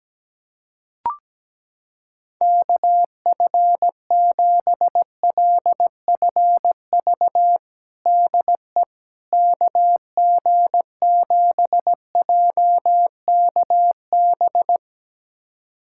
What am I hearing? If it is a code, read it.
Morse code, heard as a keyed tone, about 17 wpm, 700 Hz. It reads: KF7LFV DE KG7JKB
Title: USA Callsign DE Callsign 17wpm